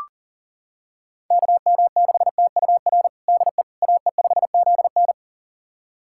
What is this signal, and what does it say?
Morse code, heard as a keyed tone, about 37 wpm, 700 Hz. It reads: KM6TUR DE AE5ZN
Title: USA Callsign DE Callsign 40wpm